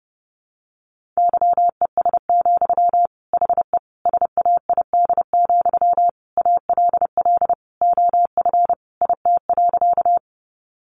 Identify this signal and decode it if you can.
Morse code, heard as a keyed tone, about 30 wpm, 700 Hz. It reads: YES, HE SAID, ALL OF IT.
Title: Sentences from Top 500 Words 30wpm